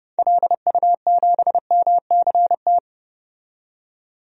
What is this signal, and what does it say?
Morse code, heard as a keyed tone, about 29 wpm, 700 Hz. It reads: LU7MCT
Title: DXCC-100 Call Signs - 6 Characters - 30wpm